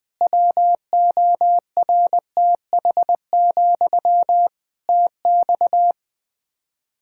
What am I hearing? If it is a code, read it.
Morse code, heard as a keyed tone, about 20 wpm, 700 Hz. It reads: WORTH, TX
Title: Practice QSO - Exchange 1 - Beginning - 20wpm